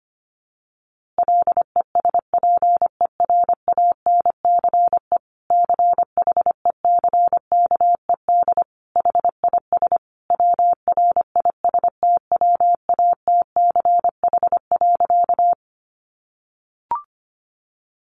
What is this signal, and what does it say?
Morse code, heard as a keyed tone, about 25 wpm, 700 Hz. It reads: LESPERANCE CHECKED HIS WRISTWATCH.
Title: A Sound Of Thunder 25wpm